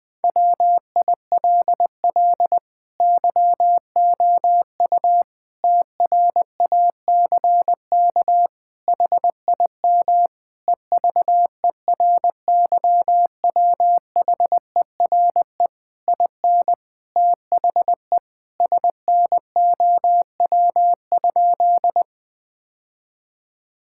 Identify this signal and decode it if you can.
Morse code, heard as a keyed tone, about 20 wpm, 700 Hz. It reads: WILL YOU TRACK HIM EVERYWHERE IN THE SNOW?
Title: Sentences from Top 1600 Words 20wpm